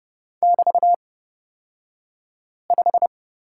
Morse code, heard as a keyed tone, about 30 wpm, spaced as for 5 wpm, 700 Hz. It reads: <BT>5
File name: Lesson 37d - Letter x - Two Characters 30wpm